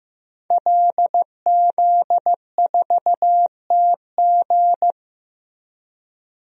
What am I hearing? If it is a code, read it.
Morse code, heard as a keyed tone, about 15 wpm, 700 Hz. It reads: LZ4TG
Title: DXCC-100 Call Signs - 5 Characters - Repeated 2x - 15wpm